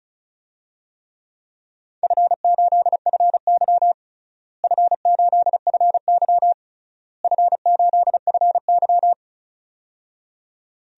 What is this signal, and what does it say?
Morse code, heard as a keyed tone, about 36 wpm, 700 Hz. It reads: F8FY F8FY F8FY
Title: DXCC-100 Call Signs - 4 Characters - Repeated 3x - 35wpm